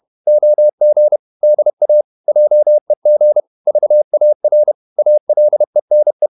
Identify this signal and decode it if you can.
Morse code, heard as a keyed tone, about 31 wpm, 600 Hz. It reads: OG DA JEG VAR ALENE
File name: Ferie QRQ 31wpm